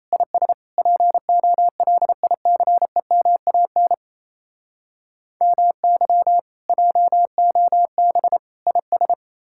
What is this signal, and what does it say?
Morse code, heard as a keyed tone, about 37 wpm, 700 Hz. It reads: IS POLICEMAN MY JOB IS